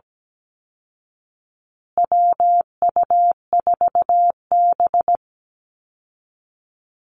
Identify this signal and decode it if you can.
Morse code, heard as a keyed tone, about 17 wpm, 700 Hz. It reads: WU4B
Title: Call Signs Repeated 2x - 17wpm